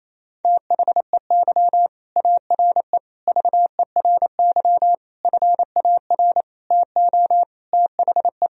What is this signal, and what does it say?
Morse code, heard as a keyed tone, about 28 wpm, 700 Hz. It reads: THEY ARE VERY FAR TO THE
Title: Sentences from Top 1000 Words - Review Part 2 - 28wpm